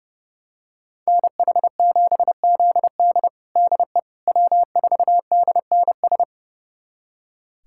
Morse code, heard as a keyed tone, about 30 wpm, 700 Hz. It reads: NH7ZD DE W4DNS
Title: USA Callsign DE Callsign - 5 Characters - 30wpm